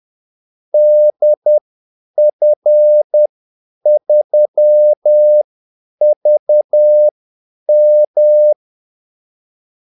Morse code, heard as a keyed tone, about 10 wpm, spaced as for 8 wpm, 600 Hz. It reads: DF3VM
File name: Callsigns-10-8wpm